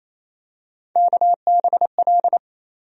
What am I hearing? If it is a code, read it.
Morse code, heard as a keyed tone, about 28 wpm, 700 Hz. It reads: KBL
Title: USA Call Sign Postfixes - 3 Letters 28wpm